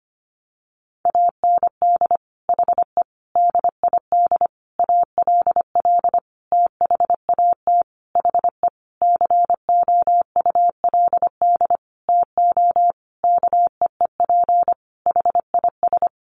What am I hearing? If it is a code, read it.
Morse code, heard as a keyed tone, about 25 wpm, 700 Hz. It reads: AND HE DID ALL THAT HE COULD TO KEEP HIS
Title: Sentences from Top 1500 Words 25wpm